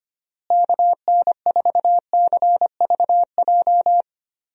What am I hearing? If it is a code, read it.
Morse code, heard as a keyed tone, about 25 wpm, 700 Hz. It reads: KN4CVJ
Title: USA Call Signs - 6 Characters - Repeated 2x - 25wpm